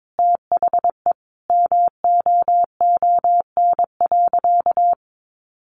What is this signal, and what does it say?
Morse code, heard as a keyed tone, about 22 wpm, 700 Hz. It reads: THE MOON.
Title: A Sound Of Thunder 22wpm